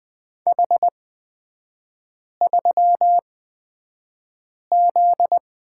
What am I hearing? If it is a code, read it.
Morse code, heard as a keyed tone, about 20 wpm, spaced as for 5 wpm, 690 Hz. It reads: H3Z
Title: 255 Lesson 41e - Period - Three Characters 20wpm